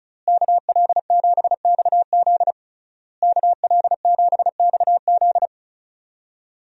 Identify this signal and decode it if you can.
Morse code, heard as a keyed tone, about 34 wpm, 700 Hz. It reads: KL7XZ KL7XZ
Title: DXCC-100 Call Signs - 5 Characters - Repeated 3x - 35wpm